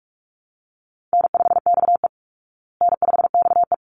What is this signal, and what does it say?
Morse code, heard as a keyed tone, about 44 wpm, 700 Hz. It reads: N5XE N5XE
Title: Easy Call Signs Repeated 2x - 45wpm